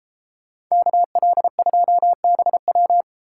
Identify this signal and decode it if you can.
Morse code, heard as a keyed tone, about 34 wpm, 700 Hz. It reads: KL2BW
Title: Speed Racing - USA 2x2 Call Signs 28wpm